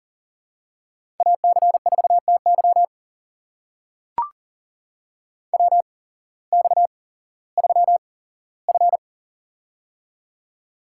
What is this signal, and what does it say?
Morse code, heard as a keyed tone, about 40 wpm, spaced as for 19 wpm, 700 Hz. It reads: AC4TY WX3F
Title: USA Call Signs - 8x Character Spacing - 40wpm